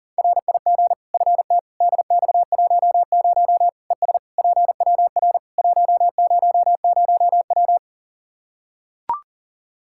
Morse code, heard as a keyed tone, about 37 wpm, 700 Hz. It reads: RIG FT DX10 ES PWR 100W
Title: K1USN SST QSO Practice 40wpm